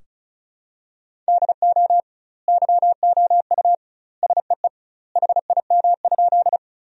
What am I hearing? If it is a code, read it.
Morse code, heard as a keyed tone, about 36 wpm, 700 Hz. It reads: DO YOU SEE HIM?
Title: 2x Word-Spacing - Sentences from Top 100 Words - 35wpm